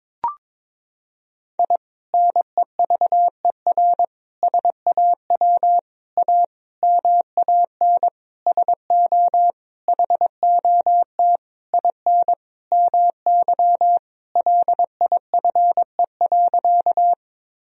Morse code, heard as a keyed tone, about 22 wpm, 700 Hz. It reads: I NEVER SAW A MAN SO HOT IN MY LIFE.
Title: Sentences from Top 1000 Words - Review Part 3 - 22wpm